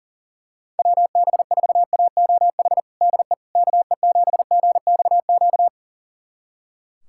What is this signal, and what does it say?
Morse code, heard as a keyed tone, about 37 wpm, 700 Hz. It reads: WB4AOH DE KE7GXQ
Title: USA Callsign DE Callsign 40wpm